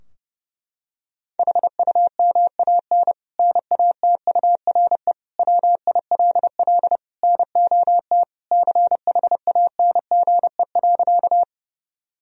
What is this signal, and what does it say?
Morse code, heard as a keyed tone, about 31 wpm, 700 Hz. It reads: HUMAN NATURE WILL NOT CHANGE.
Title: Sentences from Top 1000 Words - Review Part 3 - 30wpm